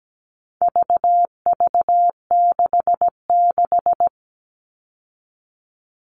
Morse code, heard as a keyed tone, about 17 wpm, 700 Hz. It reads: VV66
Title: All Cut Numbers - 4-Digit 17wpm